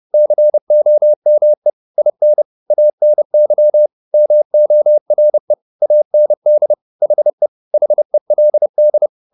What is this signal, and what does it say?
Morse code, heard as a keyed tone, about 30 wpm, 600 Hz. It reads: COME IN ANY MORE AND HE HELD